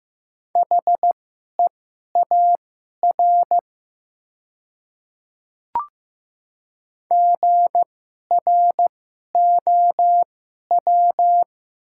Morse code, heard as a keyed tone, about 15 wpm, spaced as for 11 wpm, 700 Hz. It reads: HEAR GROW
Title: 2x Character Spacing - Top 300 Words - 15wpm